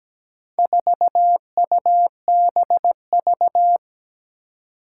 Morse code, heard as a keyed tone, about 17 wpm, 700 Hz. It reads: 4UBV
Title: Four Challenging Letter-Number Combinations 17wpm